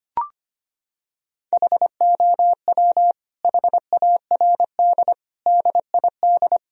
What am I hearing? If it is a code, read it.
Morse code, heard as a keyed tone, about 25 wpm, 700 Hz. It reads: HOW HARD DID
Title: Sentences from Top 1000 Words - Review Part 4 - 25wpm